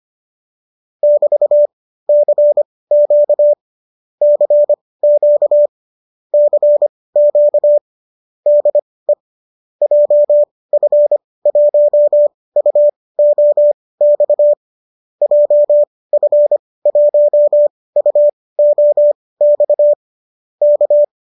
Morse code, heard as a keyed tone, about 25 wpm, spaced as for 18 wpm, 590 Hz. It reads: <BT>CQ CQ CQ DE JF1UOX JF1UOX K